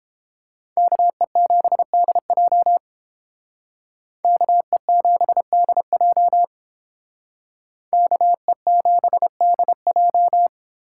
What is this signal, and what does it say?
Morse code, heard as a keyed tone, about 29 wpm, 700 Hz. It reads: KE7DJ KE7DJ KE7DJ